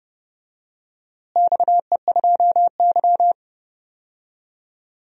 Morse code, heard as a keyed tone, about 31 wpm, 700 Hz. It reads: XE2Y
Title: DXCC-100 Call Signs - 4 Characters - Repeated 3x - 30wpm